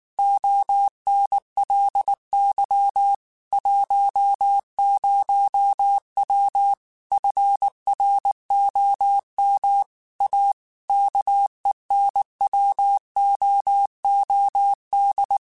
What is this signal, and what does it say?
Morse code, heard as a keyed tone, about 19 wpm, 800 Hz. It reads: ONLY 10W FROM A KENWOOD
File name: QSOs / QSOs-mp3s / WA7QOX 2@18